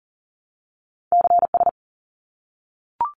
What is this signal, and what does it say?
Morse code, heard as a keyed tone, about 40 wpm, 700 Hz. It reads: CS
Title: Abbreviations 40wpm